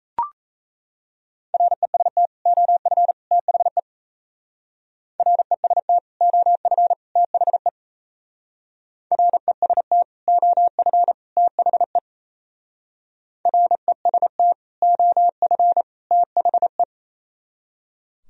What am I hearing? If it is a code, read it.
Morse code, heard as a keyed tone, about 34 wpm, 700 Hz. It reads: REST OF THE REST OF THE REST OF THE REST OF THE
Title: Speed-Racing - Sets of 3 Words 28wpm